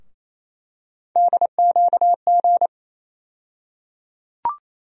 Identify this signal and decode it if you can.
Morse code, heard as a keyed tone, about 28 wpm, 700 Hz. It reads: DQG
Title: USA Call Sign Postfixes - 3 Letters 28wpm